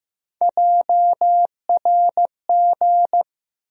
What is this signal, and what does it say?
Morse code, heard as a keyed tone, about 15 wpm, 700 Hz. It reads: JRG
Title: USA Call Sign Postfixes - 3 Letters 15wpm